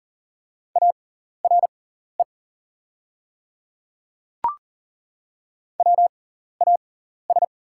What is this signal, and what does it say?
Morse code, heard as a keyed tone, about 40 wpm, spaced as for 14 wpm, 690 Hz. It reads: ARE WAS
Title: 6x Character Spacing - Top 100 Words - 40wpm